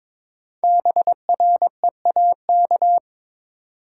Morse code, heard as a keyed tone, about 22 wpm, 700 Hz. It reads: BREAK
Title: Sets of 4 Words - New Word Drill - 22wpm